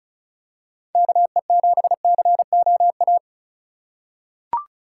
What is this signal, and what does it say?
Morse code, heard as a keyed tone, about 36 wpm, 700 Hz. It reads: KE7COA
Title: USA Call Signs - 6 Characters - 4x Character Spacing - 35wpm